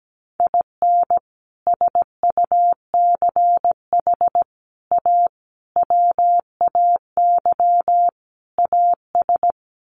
Morse code, heard as a keyed tone, about 17 wpm, 700 Hz. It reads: IN SUCH A WAY AS
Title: Sets of 5 Words Encore 17wpm